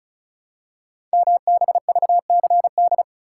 Morse code, heard as a keyed tone, about 36 wpm, 700 Hz. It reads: MBVCD
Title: Code Groups - 5 Letters - 35wpm